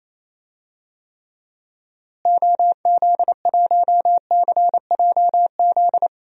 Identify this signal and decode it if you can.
Morse code, heard as a keyed tone, about 27 wpm, 700 Hz. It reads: OZ1CJZ
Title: DXCC-100 Call Signs - 6 Characters - Repeated 3x - 28wpm